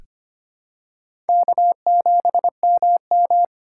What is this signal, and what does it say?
Morse code, heard as a keyed tone, about 25 wpm, 700 Hz. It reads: K7MM
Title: USA Call Signs - 3x Character Spacing - 25wpm